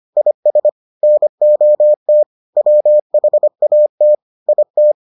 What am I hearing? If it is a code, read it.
Morse code, heard as a keyed tone, about 25 wpm, 600 Hz. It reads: IS NOT WHAT IT